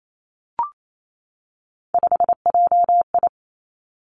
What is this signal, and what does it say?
Morse code, heard as a keyed tone, about 28 wpm, 700 Hz. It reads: 5JI
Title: Three Challenging Letter-Number Combinations 28wpm